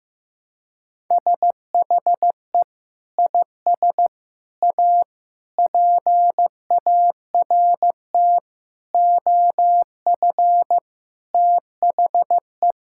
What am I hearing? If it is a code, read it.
Morse code, heard as a keyed tone, about 15 wpm, 700 Hz. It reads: SHE IS A PART OF THE
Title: Sentences from Top 1000 Words - Review Part 5 - 15wpm